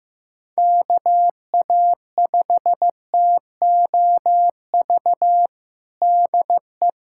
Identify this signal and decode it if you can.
Morse code, heard as a keyed tone, about 15 wpm, 700 Hz. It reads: KA5TOV DE
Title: USA Callsign DE Callsign - 6 Characters - 15wpm